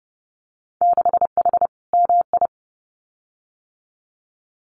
Morse code, heard as a keyed tone, about 30 wpm, 700 Hz. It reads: 6H MI
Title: Winter Field Day Exchanges Encore 30wpm